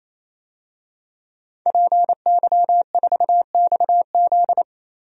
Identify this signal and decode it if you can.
Morse code, heard as a keyed tone, about 28 wpm, 700 Hz. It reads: PY4XZ
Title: DXCC-100 Call Signs - 5 Characters - 28wpm